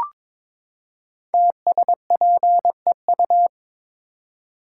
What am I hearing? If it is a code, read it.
Morse code, heard as a keyed tone, about 22 wpm, 690 Hz. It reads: TSPEU
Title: Code Groups - 5 Letter:Numbers - 22wpm